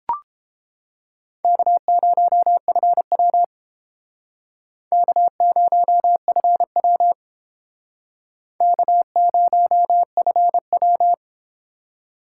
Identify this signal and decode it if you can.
Morse code, heard as a keyed tone, about 30 wpm, 700 Hz. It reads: K0FW K0FW K0FW